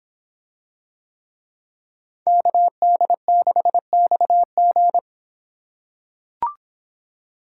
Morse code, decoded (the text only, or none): KD6XG